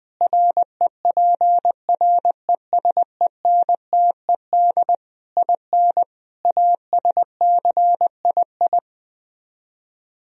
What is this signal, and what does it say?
Morse code, heard as a keyed tone, about 20 wpm, 700 Hz. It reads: REPRESENTED IN ASCII
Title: Speed-Racing - Sets of 5 Words 17wpm